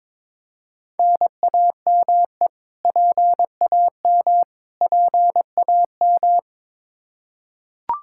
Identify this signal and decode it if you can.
Morse code, heard as a keyed tone, about 22 wpm, 700 Hz. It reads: NAME PAM PAM
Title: Practice QSO - Exchange 1 - Beginning - 22wpm